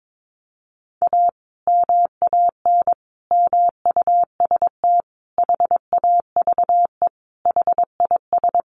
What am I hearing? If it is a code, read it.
Morse code, heard as a keyed tone, about 22 wpm, 700 Hz. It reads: A MAN MUST HAVE HIS
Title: Sentences from Top 1200 Words 22wpm